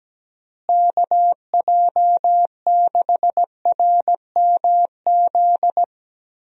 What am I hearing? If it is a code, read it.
Morse code, heard as a keyed tone, about 17 wpm, 700 Hz. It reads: KJ6RMZ
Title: USA Call Signs - 6 Characters - Repeated 3x - 17wpm